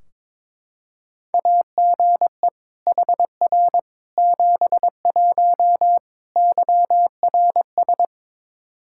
Morse code, heard as a keyed tone, about 22 wpm, 700 Hz. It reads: AGE HR 71 YRS
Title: Practice QSO - Exchange 3 - More Info - 22wpm